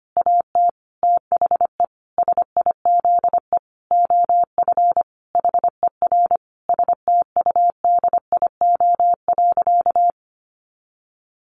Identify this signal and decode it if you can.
Morse code, heard as a keyed tone, about 25 wpm, 700 Hz. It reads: AT THE SIZE OF HER STUDIO.
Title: Sentences from Top 1200 Words 25wpm